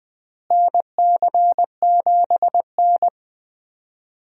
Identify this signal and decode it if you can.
Morse code, heard as a keyed tone, about 21 wpm, 700 Hz. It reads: NC7N
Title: USA Call Signs - 4 Characters - Repeated 3x - 20wpm